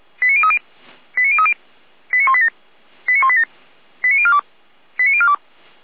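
Five-tone selective call (ZVEI), heard as quick runs of five tones, tones of about 70 ms. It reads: 89030 89030 89187 89187 89042 89042
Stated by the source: Five-tone ZVEI selective call received on an SDR.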